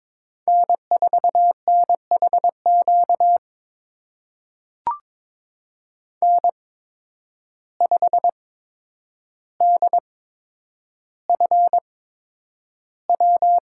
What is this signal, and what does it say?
Morse code, heard as a keyed tone, about 21 wpm, spaced as for 9 wpm, 700 Hz. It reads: N4NHQ N5DFW
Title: USA Call Signs - 8x Character Spacing - 22wpm